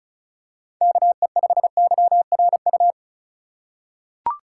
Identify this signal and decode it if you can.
Morse code, heard as a keyed tone, about 34 wpm, 700 Hz. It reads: KE5YRU
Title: USA Call Signs - 6 Characters - 2x Character Spacing - 35wpm